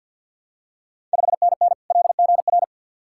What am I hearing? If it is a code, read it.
Morse code, heard as a keyed tone, about 49 wpm, 690 Hz. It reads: 5NN LGR
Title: Ontario QSO Party Exchange 50wpm